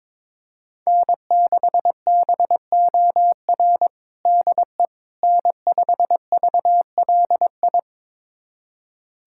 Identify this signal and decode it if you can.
Morse code, heard as a keyed tone, about 22 wpm, 700 Hz. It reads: N6BOR DE N5VLI
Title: USA Callsign DE Callsign - 5 Characters - 22wpm